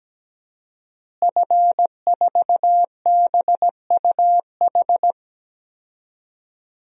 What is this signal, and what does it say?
Morse code, heard as a keyed tone, about 17 wpm, 700 Hz. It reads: F4BUH
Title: DXCC-100 Call Signs - 5 Characters - Repeated 3x - 17wpm